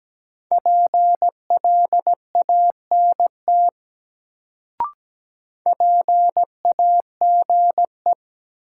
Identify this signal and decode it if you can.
Morse code, heard as a keyed tone, about 17 wpm, 700 Hz. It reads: PLANT PAGE
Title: Sentences from Top 1000 Words - Review Part 1 - New Word Drill - 17wpm